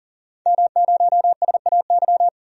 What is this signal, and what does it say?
Morse code, heard as a keyed tone, about 40 wpm, 700 Hz. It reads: M0SAY
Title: DXCC-100 Call Signs - 5 Characters - 40wpm